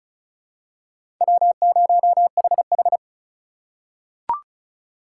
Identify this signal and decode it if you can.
Morse code, heard as a keyed tone, about 36 wpm, 700 Hz. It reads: W0HH
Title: USA Call Signs - 4 Characters - 2x Character Spacing - 35wpm